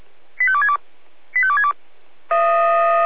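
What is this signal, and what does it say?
Five-tone selective call (ZVEI), heard as quick runs of five tones, tones of about 70 ms. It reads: 85262 85262
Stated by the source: SDR-received ZVEI five-tone selective call, code 85270.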